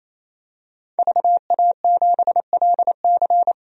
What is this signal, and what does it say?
Morse code, heard as a keyed tone, about 28 wpm, 700 Hz. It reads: VA7LC
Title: DXCC-100 Call Signs - 5 Characters - 28wpm